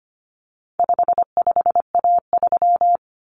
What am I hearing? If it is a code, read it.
Morse code, heard as a keyed tone, about 25 wpm, 700 Hz. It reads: 55A3
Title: Common Cut Numbers - 4-Digit 25wpm